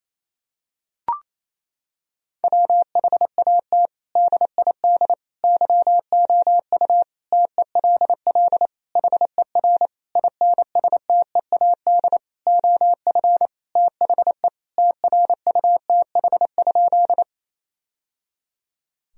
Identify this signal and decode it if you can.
Morse code, heard as a keyed tone, about 28 wpm, 700 Hz. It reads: WHAT DID YOU TELL HER INSTEAD OF THE TRUTH?
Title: Sentences from Top 1000 Words - Review Part 4 - 28wpm